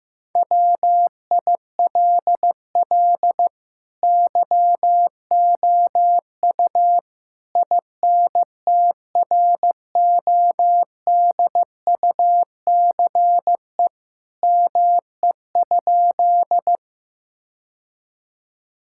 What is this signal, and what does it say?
Morse code, heard as a keyed tone, about 15 wpm, 700 Hz. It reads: WILL YOU INTRODUCE ME?
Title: Sentences from Top 1600 Words 15wpm